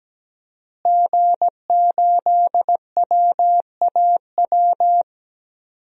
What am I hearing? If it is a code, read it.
Morse code, heard as a keyed tone, about 17 wpm, 700 Hz. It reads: G8WAW
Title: DXCC-100 Call Signs - 5 Characters - Repeated 3x - 17wpm